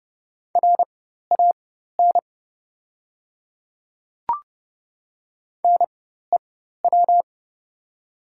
Morse code, heard as a keyed tone, about 30 wpm, spaced as for 14 wpm, 690 Hz. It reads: RAN NEW
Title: 4x Character Spacing - Top 500 Words - 30wpm